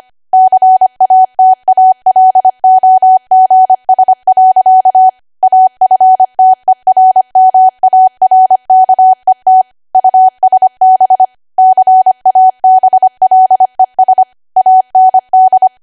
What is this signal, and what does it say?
Morse code, heard as a keyed tone, about 25 wpm, 750 Hz. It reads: CATALOGS. AFTERMARKET USB CABLES AND
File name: Archive / 25 WPM / 25WPM